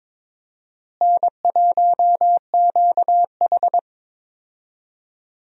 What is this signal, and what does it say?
Morse code, heard as a keyed tone, about 22 wpm, 700 Hz. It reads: N1QH
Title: USA Call Signs - 4 Characters - Repeated 3x - 22wpm